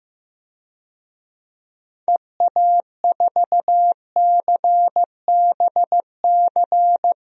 Morse code, heard as a keyed tone, about 15 wpm, 700 Hz. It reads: EA4CBC
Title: DXCC-100 Call Signs - 6 Characters - Repeated 2x - 15wpm